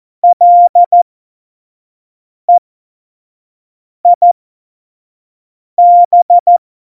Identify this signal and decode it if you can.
Morse code, heard as a keyed tone, about 14 wpm, spaced as for 5 wpm, 700 Hz. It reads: LEIB
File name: Morsekurs / Witze / Witz 05. Januar 2021, 25 F70